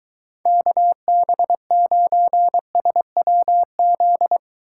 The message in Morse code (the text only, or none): KB9SWZ